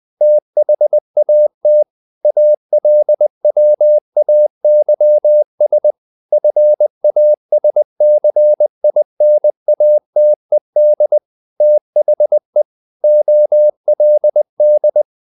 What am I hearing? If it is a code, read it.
Morse code, heard as a keyed tone, about 20 wpm, 600 Hz. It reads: THAT ALWAYS FASCINATED THE OLD
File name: the old man and the sea part 2 20wpm